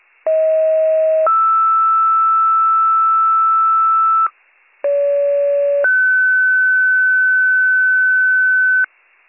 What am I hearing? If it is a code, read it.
Two-tone page, heard as a one-second tone, then a three-second tone, about 640 then 1320 Hz, about 570 then 1500 Hz, with noise. It is heard from 0.3 to 8.8 s.